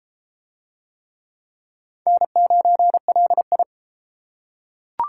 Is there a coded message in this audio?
Yes: Morse code from 2.1 to 3.6 s, heard as a keyed tone, about 31 wpm, 700 Hz. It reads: N9LI